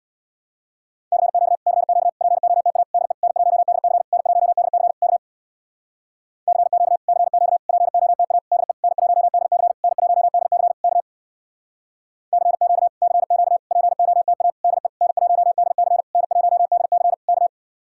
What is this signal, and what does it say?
Morse code, heard as a keyed tone, about 63 wpm, 700 Hz. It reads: CQ CQ POTA DE N1DC N1DC K CQ CQ POTA DE N1DC N1DC K CQ CQ POTA DE N1DC N1DC K